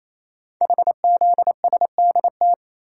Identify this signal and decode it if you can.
Morse code, heard as a keyed tone, about 28 wpm, 700 Hz. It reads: HZSDT
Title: Code Groups - 5 Letters - 28wpm